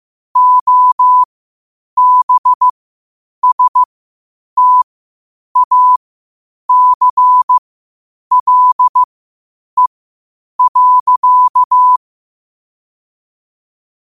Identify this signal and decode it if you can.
Morse code, heard as a keyed tone, about 15 wpm, spaced as for 8 wpm, 1000 Hz. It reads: OBSTACLE.
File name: qotdfiles / 8wpm